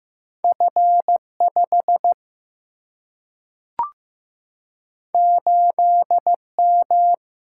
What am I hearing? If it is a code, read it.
Morse code, heard as a keyed tone, about 15 wpm, 700 Hz. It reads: F 5 8 M
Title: Single-Letter Single-Number 15wpm